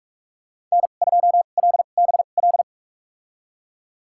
Morse code, heard as a keyed tone, about 46 wpm, spaced as for 33 wpm, 700 Hz. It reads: N2LBL
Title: USA Call Signs - 2x Character Spacing - 45wpm